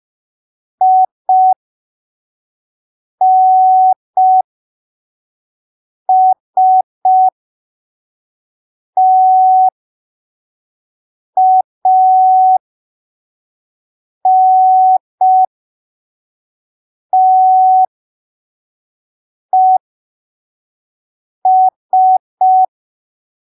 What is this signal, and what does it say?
Morse code, heard as a keyed tone, about 5 wpm, spaced as for 3 wpm, 750 Hz. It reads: INSTANTES